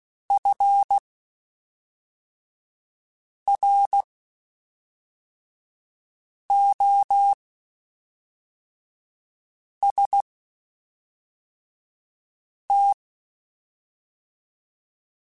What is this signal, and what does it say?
Morse code, heard as a keyed tone, about 16 wpm, spaced as for 3 wpm, 800 Hz. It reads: FROST
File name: Spaced Characters / Spaced Characters-mp3s / Weather4 24 sp @5